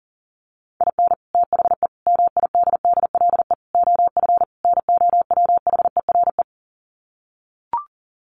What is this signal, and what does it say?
Morse code, heard as a keyed tone, about 40 wpm, 700 Hz. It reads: IN THE MIDDLE OF NOWHERE
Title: Sets of 5 Words Encore 40wpm